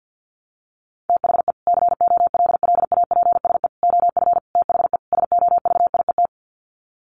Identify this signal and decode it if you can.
Morse code, heard as a keyed tone, about 50 wpm, 700 Hz. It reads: THE COLLAPSE OF THE SOVIET
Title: Sets of 5 Words Encore 50wpm